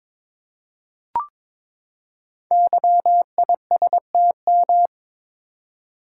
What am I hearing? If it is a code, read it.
Morse code, heard as a keyed tone, about 22 wpm, 700 Hz. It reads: YISTM
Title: Code Groups - 5 Letters - 22wpm